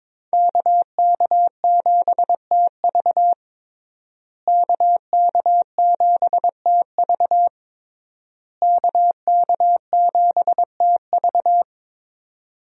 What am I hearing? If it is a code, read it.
Morse code, heard as a keyed tone, about 21 wpm, 700 Hz. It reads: KK7TV KK7TV KK7TV
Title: USA Call Signs - 5 Characters - Repeated 3x - 22wpm